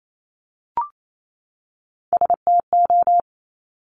Morse code, heard as a keyed tone, about 28 wpm, 700 Hz. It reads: STO
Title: Top Three Letter Combinations in Words 28wpm